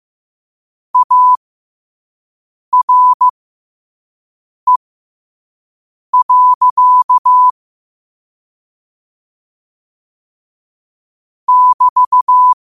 Morse code, heard as a keyed tone, about 15 wpm, spaced as for 5 wpm, 1000 Hz. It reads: ARE. <BT>
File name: qotdfiles / 5wpm